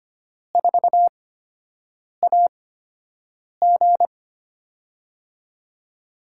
Morse code, heard as a keyed tone, about 25 wpm, spaced as for 7 wpm, 700 Hz. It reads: 4AG
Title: USA Call Signs - 5 Characters - 8x Character Spacing - 25wpm